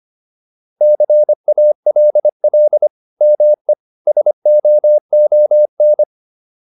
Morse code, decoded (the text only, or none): CALL ME SOON